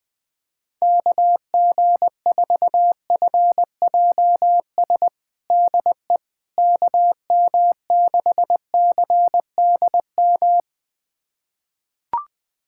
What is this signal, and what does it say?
Morse code, heard as a keyed tone, about 20 wpm, 700 Hz. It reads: KG4FJS DE KM6CDM
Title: USA Callsign DE Callsign 20wpm